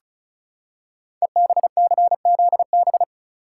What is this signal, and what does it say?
Morse code, heard as a keyed tone, about 36 wpm, 690 Hz. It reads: EBCZB
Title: Code Groups - 5 Letters - 35wpm